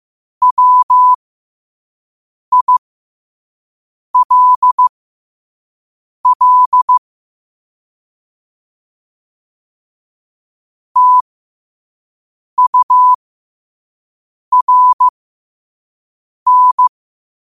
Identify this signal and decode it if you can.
Morse code, heard as a keyed tone, about 15 wpm, spaced as for 5 wpm, 1000 Hz. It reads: WILL TURN
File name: qotdfiles / 5wpm